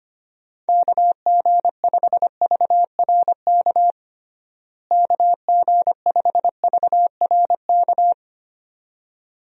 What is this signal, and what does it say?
Morse code, heard as a keyed tone, about 25 wpm, 700 Hz. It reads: KG5VRK KG5VRK
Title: USA Call Signs - 6 Characters - Repeated 2x - 25wpm